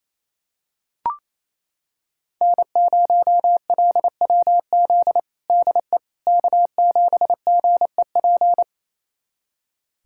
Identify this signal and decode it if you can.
Morse code, heard as a keyed tone, about 28 wpm, 700 Hz. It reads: N0LWZ DE K7GEP
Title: USA Callsign DE Callsign - 5 Characters - 28wpm